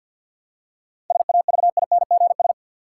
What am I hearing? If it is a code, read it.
Morse code, heard as a keyed tone, about 49 wpm, 700 Hz. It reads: SAVINGS
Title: Seven Letter Words - 50wpm